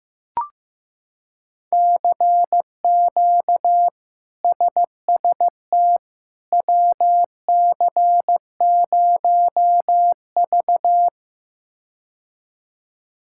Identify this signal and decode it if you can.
Morse code, heard as a keyed tone, about 15 wpm, 700 Hz. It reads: CQ SST WC0V
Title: K1USN SST QSO Practice - 15wpm